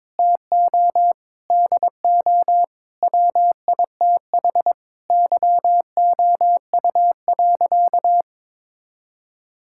Morse code, heard as a keyed tone, about 22 wpm, 700 Hz. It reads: TO DO WITH YOU.
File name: practice / 22wpm / Sentences with 4-Word Phrases - 22wpm